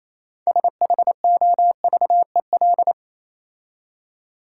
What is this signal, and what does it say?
Morse code, heard as a keyed tone, about 27 wpm, 690 Hz. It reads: SHOVEL
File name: Easy Words - Speed-Racing with Hints - 22wpm